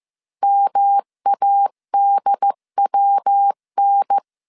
Morse code, heard as a keyed tone, about 15 wpm, 800 Hz. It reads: MADWN
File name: CW / DET / 1 3 1500C S C 14wpm 2458